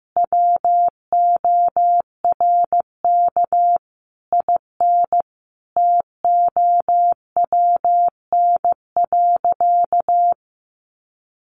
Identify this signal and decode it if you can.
Morse code, heard as a keyed tone, about 15 wpm, 700 Hz. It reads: WORK IN TOWN.
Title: Sentences from Top 1500 Words 15wpm